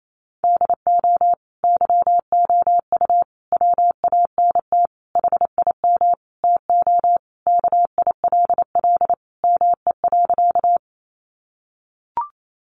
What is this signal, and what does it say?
Morse code, heard as a keyed tone, about 28 wpm, 700 Hz. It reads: DO YOU WANT HIM TO KILL ME.
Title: Sentences from Top 1400 Words 28wpm